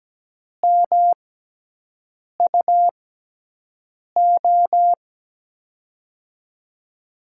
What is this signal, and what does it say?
Morse code, heard as a keyed tone, about 17 wpm, spaced as for 6 wpm, 700 Hz. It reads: MUO
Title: USA Call Signs - 6x Character Spacing - 17wpm